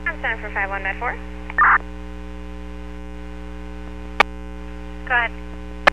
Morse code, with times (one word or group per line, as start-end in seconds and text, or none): none